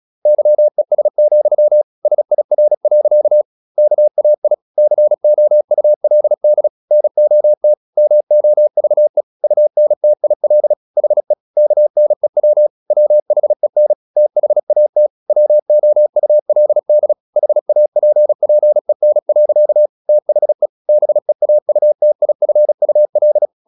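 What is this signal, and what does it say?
Morse code, heard as a keyed tone, about 36 wpm, 600 Hz. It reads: YES, SIR. KAI COULD NOT MOVE UNTIL HE KNEW WHEN THAT WOULD HAPPEN. THE BEAUTIFUL
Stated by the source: Morse practice at 35 wpm from CW med Gnister, sending Ken Follett's novel Never in English.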